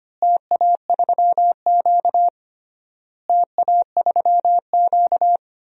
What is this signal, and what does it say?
Morse code, heard as a keyed tone, about 25 wpm, 700 Hz. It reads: TA3Q TA3Q
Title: DXCC-100 Call Signs - 4 Characters - Repeated 3x - 25wpm